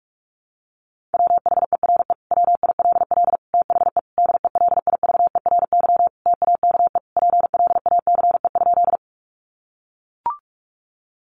Morse code, heard as a keyed tone, about 44 wpm, 700 Hz. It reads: WHERE WILL THE DELIVERY TAKE PLACE?
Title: Sentences from Top 1500 Words 45wpm